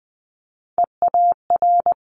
Morse code, heard as a keyed tone, about 19 wpm, 700 Hz. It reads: EAR
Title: Multi-letter Phonemes Quick Fire 20wpm